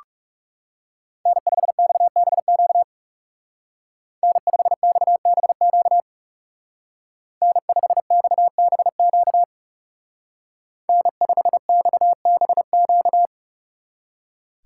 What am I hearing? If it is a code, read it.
Morse code, heard as a keyed tone, about 35 wpm, 700 Hz. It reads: N5XBQ N5XBQ N5XBQ N5XBQ
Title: Speed Racing - USA Call Signs 30wpm